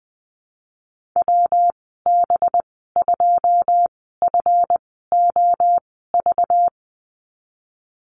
Morse code, heard as a keyed tone, about 20 wpm, spaced as for 15 wpm, 700 Hz. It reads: WB2FOV
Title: 193 Lesson 31g - Letter v - Call Signs 20wpm